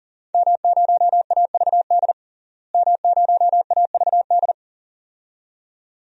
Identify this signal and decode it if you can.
Morse code, heard as a keyed tone, about 40 wpm, 700 Hz. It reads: M0AVD M0AVD
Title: DXCC-100 Call Signs - 5 Characters - Repeated 2x - 40wpm